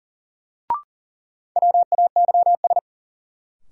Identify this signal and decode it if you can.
Morse code, heard as a keyed tone, about 40 wpm, 700 Hz. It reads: WAYS
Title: Sets of 5 Words Encore - New Word Drill - 40wpm